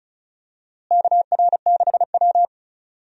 Morse code, heard as a keyed tone, about 36 wpm, 700 Hz. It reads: KR6W
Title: USA Call Signs - 4 Characters - Repeated 2x - 35wpm